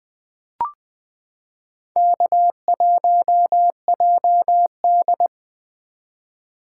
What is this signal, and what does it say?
Morse code, heard as a keyed tone, about 21 wpm, 700 Hz. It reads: K1JD
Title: 2023 SOTA Activator Call Signs - 20wpm